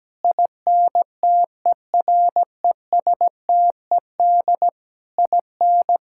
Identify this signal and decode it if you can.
Morse code, heard as a keyed tone, about 17 wpm, 700 Hz. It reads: INTERESTED IN
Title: Sentences from Top 1000 Words - Review Part 6 - 17wpm